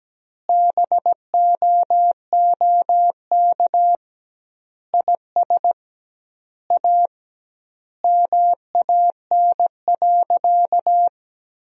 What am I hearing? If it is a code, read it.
Morse code, heard as a keyed tone, about 17 wpm, 700 Hz. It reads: BOOK IS A MAN.
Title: 2x Word-Spacing - Sentences from Top 500 Words - 17wpm